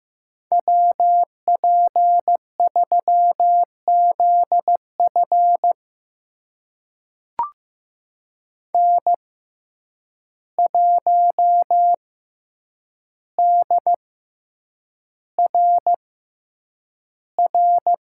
Morse code, heard as a keyed tone, about 15 wpm, spaced as for 8 wpm, 700 Hz. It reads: WP3ZF N1DRR
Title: USA Call Signs - 5 Characters - 6x Character Spacing - 15wpm